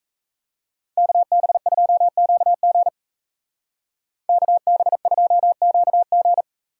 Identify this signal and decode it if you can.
Morse code, heard as a keyed tone, about 37 wpm, 700 Hz. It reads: KB2QG KB2QG